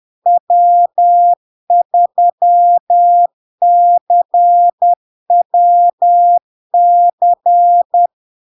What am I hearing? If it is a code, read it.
Morse code, heard as a keyed tone, about 10 wpm, 700 Hz. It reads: W3CWC
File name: LARK-FD-2022-Log-in-CW-10wpm-700-Hz